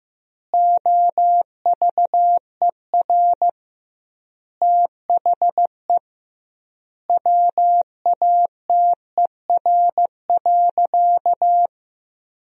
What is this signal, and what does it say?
Morse code, heard as a keyed tone, about 15 wpm, 700 Hz. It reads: OVER THE WATER.
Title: 2x Word-Spacing - Sentences from Top 300 Words - 15wpm